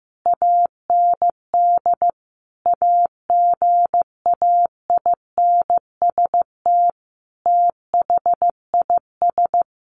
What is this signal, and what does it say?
Morse code, heard as a keyed tone, about 14 wpm, 700 Hz. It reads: AND AGAINST THIS
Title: Sentences from Top 1300 Words 15wpm